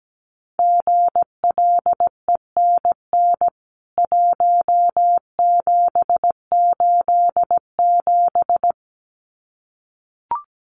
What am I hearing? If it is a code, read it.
Morse code, heard as a keyed tone, about 17 wpm, 700 Hz. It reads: GLENN 1787
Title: CWT Exchange 17wpm